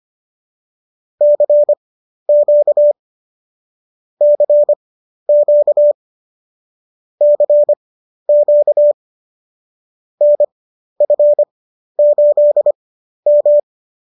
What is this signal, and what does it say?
Morse code, heard as a keyed tone, about 25 wpm, spaced as for 12 wpm, 590 Hz. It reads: CQ CQ CQ NF8M